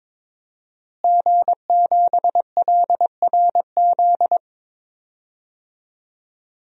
Morse code, heard as a keyed tone, about 22 wpm, 700 Hz. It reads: G7LRZ
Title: DXCC-100 Call Signs - 5 Characters - Repeated 2x - 22wpm